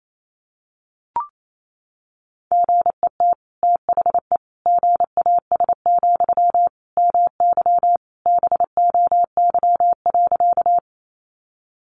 Morse code, heard as a keyed tone, about 28 wpm, 700 Hz. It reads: GET THE GAS, MY BOY.
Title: Sentences from Top 1100 Words 28wpm